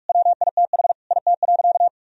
Morse code, heard as a keyed tone, about 46 wpm, 700 Hz. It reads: WITH IT.
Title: Sentences from Top 1000 Words - Review Part 3 - 45wpm